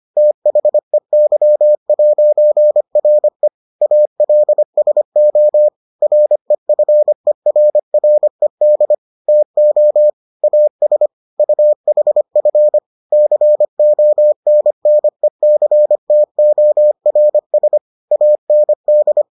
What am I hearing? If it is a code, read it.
Morse code, heard as a keyed tone, about 25 wpm, 600 Hz. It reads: THEY'RE ALSO REFERRED TO AS UHF CONNECTORS AND